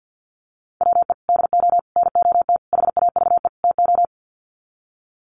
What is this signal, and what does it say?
Morse code, heard as a keyed tone, about 50 wpm, 700 Hz. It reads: WE DO NOT HAVE TO
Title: Sets of 5 Words 50wpm